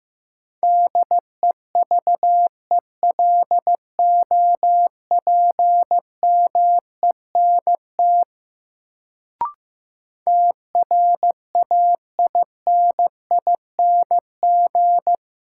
Morse code, heard as a keyed tone, about 16 wpm, 700 Hz. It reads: DEVELOPMENT TRAINING
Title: Sentences from Top 1000 Words - Review Part 4 - New Word Drill - 15wpm